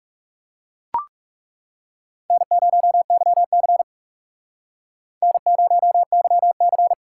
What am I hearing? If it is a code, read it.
Morse code, heard as a keyed tone, about 37 wpm, 700 Hz. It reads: N0YC N0YC
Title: Speed Racing - USA 1x2 Call Signs 30wpm